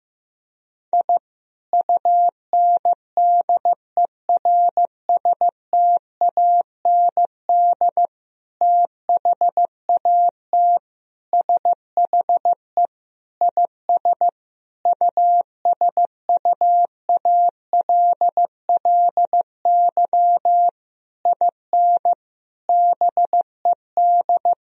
Morse code, heard as a keyed tone, about 15 wpm, 700 Hz. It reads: I UNDERSTAND THAT SHE IS USUALLY IN BED
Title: Sentences from Top 1000 Words - Review Part 3 - 15wpm